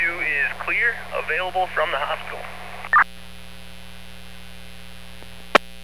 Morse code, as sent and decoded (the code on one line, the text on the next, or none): none